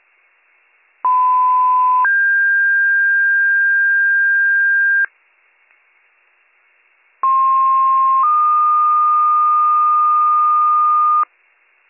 Two-tone page, heard as a one-second tone, then a three-second tone, about 1010 then 1600 Hz, about 1060 then 1220 Hz. It is heard from 1.0 to 11.2 s.